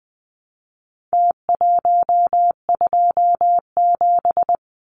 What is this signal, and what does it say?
Morse code, heard as a keyed tone, about 20 wpm, 700 Hz. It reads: T127
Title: Numbers - Contest Style - 4-Digit 20wpm